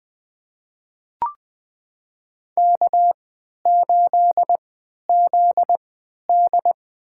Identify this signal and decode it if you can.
Morse code, heard as a keyed tone, about 20 wpm, spaced as for 11 wpm, 700 Hz. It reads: K8ZD
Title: USA Call Signs - 3x Character Spacing - 20wpm